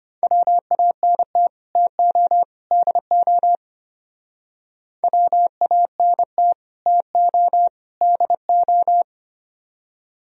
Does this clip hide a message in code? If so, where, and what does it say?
Yes: Morse code from 0.2 to 9.0 s, heard as a keyed tone, about 33 wpm, 700 Hz. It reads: WANT TO DO WANT TO DO